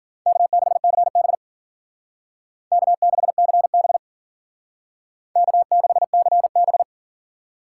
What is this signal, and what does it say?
Morse code, heard as a keyed tone, about 50 wpm, 700 Hz. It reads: K6CB K6CB K6CB
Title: Speed Racing - USA Call Signs 40wpm